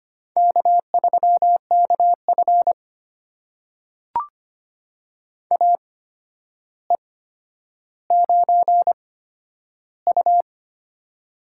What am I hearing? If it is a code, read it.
Morse code, heard as a keyed tone, about 25 wpm, spaced as for 11 wpm, 700 Hz. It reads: K3KF AE9U
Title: USA Call Signs - 4 Characters - 8x Character Spacing - 25wpm